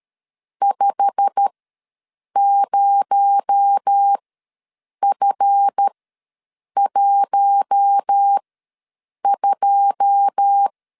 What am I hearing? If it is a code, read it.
Morse code, heard as a keyed tone, about 13 wpm, spaced as for 7 wpm, 800 Hz. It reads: 50F12
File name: LEZIONI / 10 SIGNS / 10A SLASH ws 12wpm 3443